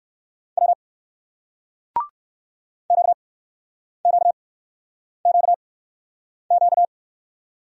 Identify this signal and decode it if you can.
Morse code, heard as a keyed tone, about 55 wpm, spaced as for 9 wpm, 700 Hz. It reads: W QQQQ
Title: Speed-Racing Single Letter-Number 45wpm